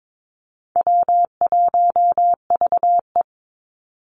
Morse code, heard as a keyed tone, about 22 wpm, 700 Hz. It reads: W1VE
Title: Call Signs - 22wpm